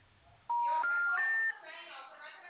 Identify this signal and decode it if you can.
Special information tone, with speech.
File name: out- -2013- - - .1219